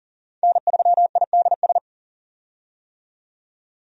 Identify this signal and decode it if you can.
Morse code, heard as a keyed tone, about 40 wpm, 700 Hz. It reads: N3IDS
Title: USA Call Signs - 5 Characters - 40wpm